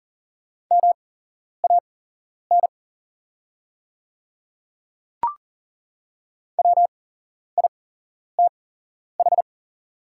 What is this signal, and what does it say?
Morse code, heard as a keyed tone, about 40 wpm, spaced as for 11 wpm, 700 Hz. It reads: MAN WITH
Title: 8x Character Spacing - Top 200 Words - 40wpm